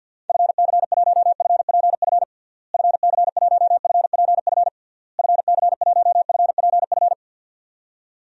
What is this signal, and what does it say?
Morse code, heard as a keyed tone, about 52 wpm, 700 Hz. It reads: FC1FPF FC1FPF FC1FPF
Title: DXCC-100 Call Signs - 6 Characters - Repeated 3x - 50wpm